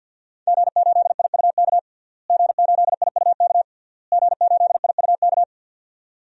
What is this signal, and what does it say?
Morse code, heard as a keyed tone, about 52 wpm, 700 Hz. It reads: G8IUK G8IUK G8IUK
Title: DXCC-100 Call Signs - 5 Characters - Repeated 3x - 50wpm